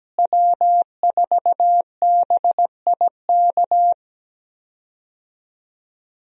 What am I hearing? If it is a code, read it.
Morse code, heard as a keyed tone, about 17 wpm, 700 Hz. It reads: W4BIK
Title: USA Call Signs - 5 Characters - Repeated 2x - 17wpm